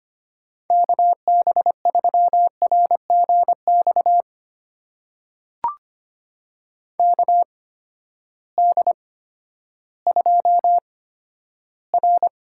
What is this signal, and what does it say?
Morse code, heard as a keyed tone, about 25 wpm, spaced as for 12 wpm, 700 Hz. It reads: KB3RGX KD2R
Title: USA Call Signs - 6 Characters - 8x Character Spacing - 25wpm